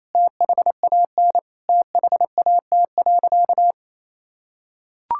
Morse code, heard as a keyed tone, about 28 wpm, 700 Hz. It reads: THAN THAT.
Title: Sentences from Top 1600 Words 28wpm